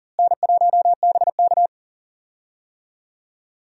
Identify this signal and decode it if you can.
Morse code, heard as a keyed tone, about 40 wpm, 700 Hz. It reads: N1BK
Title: USA Call Signs - 4 Characters - Repeated 2x - 40wpm